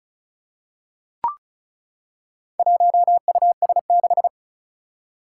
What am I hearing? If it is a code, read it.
Morse code, heard as a keyed tone, about 36 wpm, 700 Hz. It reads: 1US6
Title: Four Challenging Letter-Number Combinations 35wpm